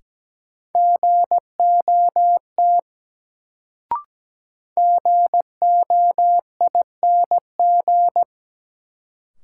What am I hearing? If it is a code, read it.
Morse code, heard as a keyed tone, about 17 wpm, 700 Hz. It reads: GOT GOING
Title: Sets of 2 Words - New Word Drill - 17wpm